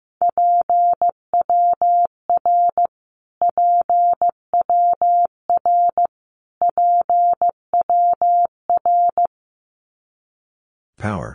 Morse code, heard as a keyed tone, about 15 wpm, 700 Hz. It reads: PWR PWR PWR
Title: Abbreviations Repeated 3x - 15wpm